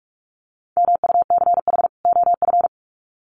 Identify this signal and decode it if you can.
Morse code, heard as a keyed tone, about 44 wpm, 700 Hz. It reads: MUCH OF
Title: Sets of 2 Words 45wpm